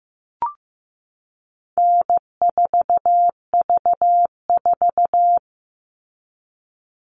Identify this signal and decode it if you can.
Morse code, heard as a keyed tone, about 15 wpm, 700 Hz. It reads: N4V4
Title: All Cut Numbers - 4-Digit 15wpm